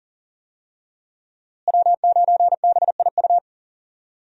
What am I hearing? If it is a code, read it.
Morse code, heard as a keyed tone, about 40 wpm, 700 Hz. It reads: W9BIU
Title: USA Call Signs - 5 Characters - Repeated 3x - 40wpm